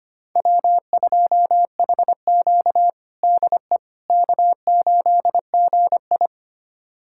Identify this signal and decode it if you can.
Morse code, heard as a keyed tone, about 25 wpm, 700 Hz. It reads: W2HQ DE K8GI
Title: USA Callsign DE Callsign - 4 Characters - 25wpm